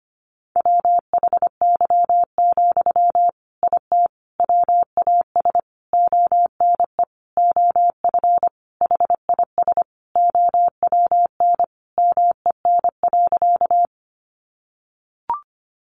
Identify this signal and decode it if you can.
Morse code, heard as a keyed tone, about 25 wpm, 700 Hz. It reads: WHY, IT WAS ONE OF HIS OWN MEN.
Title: Sentences from Top 300 Words 25wpm